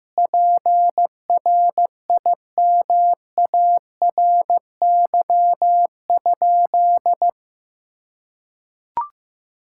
Morse code, heard as a keyed tone, about 16 wpm, 700 Hz. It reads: PRIMARY?
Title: Sentences from Top 1000 Words - Review Part 5 - 15wpm